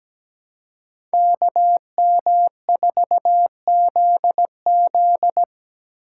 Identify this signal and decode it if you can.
Morse code, heard as a keyed tone, about 17 wpm, 700 Hz. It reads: KM4ZZ
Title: K1USN SST QSO Practice - 17wpm